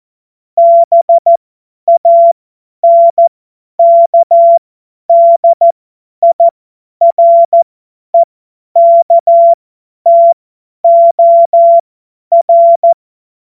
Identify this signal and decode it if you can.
Morse code, heard as a keyed tone, about 14 wpm, spaced as for 10 wpm, 680 Hz. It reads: BANKDIREKTOR
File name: Morsekurs / Witze / Witz 13 Mai 2025 50 F70